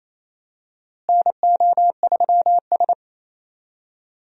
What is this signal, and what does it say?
Morse code, heard as a keyed tone, about 28 wpm, 700 Hz. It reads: NO3S
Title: USA Call Signs - 4 Characters - Repeated 3x - 28wpm